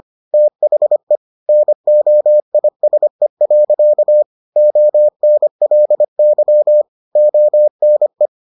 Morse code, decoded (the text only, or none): THE NOISE. ONLY ONE